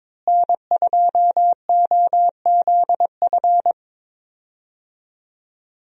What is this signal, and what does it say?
Morse code, heard as a keyed tone, about 22 wpm, 700 Hz. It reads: N2OZF
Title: USA Call Signs - 5 Characters - Repeated 2x - 22wpm